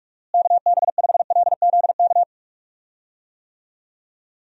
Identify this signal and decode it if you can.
Morse code, heard as a keyed tone, about 46 wpm, 700 Hz. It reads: KB5LZK
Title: USA Call Signs - 6 Characters - Repeated 2x - 45wpm